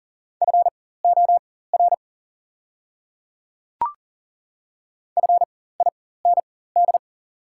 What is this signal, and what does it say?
Morse code, heard as a keyed tone, about 40 wpm, spaced as for 19 wpm, 690 Hz. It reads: FOR FIND
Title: 4x Character Spacing - Top 100 Words - 40wpm